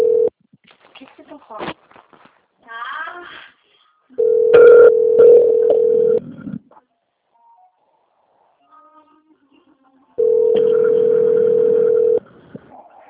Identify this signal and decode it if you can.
Ringback tone (North America), 440 + 480 Hz, with speech.
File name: out- - - - - .83919